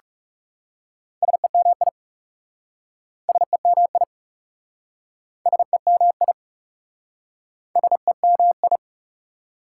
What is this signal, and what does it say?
Morse code, heard as a keyed tone, about 37 wpm, 700 Hz. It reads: SEMI SEMI SEMI SEMI
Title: Speed-Racing Prefixes and Suffixes 30wpm